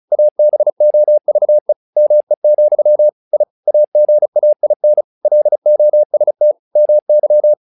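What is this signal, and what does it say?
Morse code, heard as a keyed tone, about 34 wpm, 600 Hz. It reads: ABOVE ME, I AGAIN LOST MY
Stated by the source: Morse practice text "Tower Trouble" sent at 35 WPM, from the SKCC - Straight Key Century Club learning center audio files.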